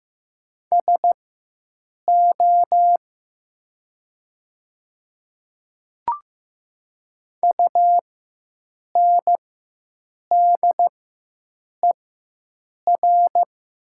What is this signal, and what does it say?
Morse code, heard as a keyed tone, about 15 wpm, spaced as for 7 wpm, 700 Hz. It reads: SO UNDER
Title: 4x Character Spacing - Top 300 Words - 15wpm